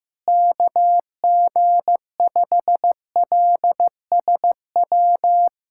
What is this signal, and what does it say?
Morse code, heard as a keyed tone, about 15 wpm, 700 Hz. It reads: KG5LSW
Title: USA Callsign DE Callsign 15wpm